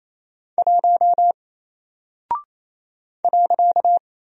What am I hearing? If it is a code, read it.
Morse code, heard as a keyed tone, about 27 wpm, 700 Hz. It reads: 1 .
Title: Single Character (Letter:Number:Punctuation:Prosign) Alt 28wpm